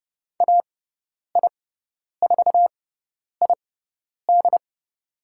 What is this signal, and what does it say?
Morse code, heard as a keyed tone, about 29 wpm, spaced as for 10 wpm, 700 Hz. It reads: AI4ID
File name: Lesson 16g - Letter c - Call Signs 30wpm